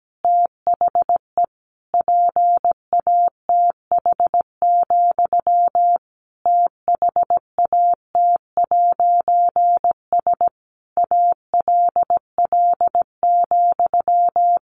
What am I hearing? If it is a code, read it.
Morse code, heard as a keyed tone, about 17 wpm, 700 Hz. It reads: THE PATH, THAT'S ALL,
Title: A Sound Of Thunder 17wpm